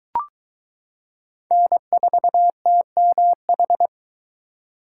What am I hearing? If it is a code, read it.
Morse code, heard as a keyed tone, about 21 wpm, 700 Hz. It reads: N4TMH